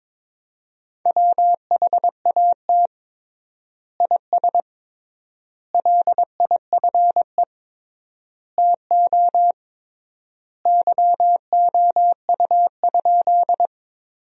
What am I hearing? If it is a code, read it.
Morse code, heard as a keyed tone, about 22 wpm, 700 Hz. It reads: WHAT IS LIFE TO YOU?
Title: 3x Word-Spacing - Sentences from Top 300 Words - 22wpm